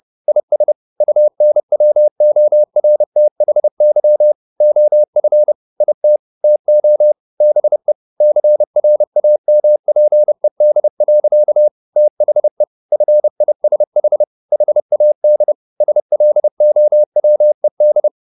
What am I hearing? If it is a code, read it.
Morse code, heard as a keyed tone, about 30 wpm, 600 Hz. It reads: IS UNWORTHY OF IT TO BE CRAMPED. THE FISH HAD SLOWED
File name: the old man and the sea / the old man and the sea 1 30wpm